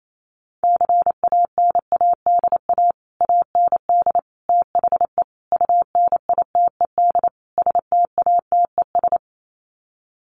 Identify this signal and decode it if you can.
Morse code, heard as a keyed tone, about 28 wpm, 700 Hz. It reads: CANADA AND THE UNITED STATES
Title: Sets of 5 Words Encore 28wpm